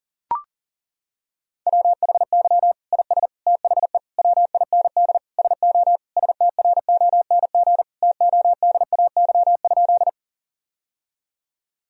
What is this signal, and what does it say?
Morse code, heard as a keyed tone, about 40 wpm, 700 Hz. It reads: WHY IS THE WIND SO STRONG TODAY?
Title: Sentences from Top 1000 Words - Review Part 2 - 40wpm